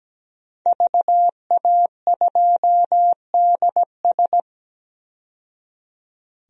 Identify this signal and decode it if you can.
Morse code, heard as a keyed tone, about 17 wpm, 700 Hz. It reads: VA2DS
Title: DXCC-100 Call Signs - 5 Characters - 17wpm